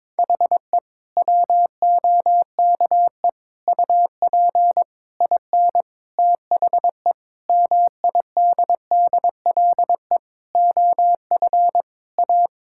Morse code, heard as a keyed tone, about 22 wpm, 700 Hz. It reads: HE WOKE UP IN THE MIDDLE OF A
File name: practice / 22wpm / Sentences with 4-Word Phrases - 22wpm